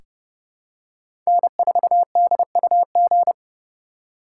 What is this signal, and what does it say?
Morse code, heard as a keyed tone, about 30 wpm, 700 Hz. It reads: N4DUG
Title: USA Call Signs - 5 Characters - 4x Character Spacing - 30wpm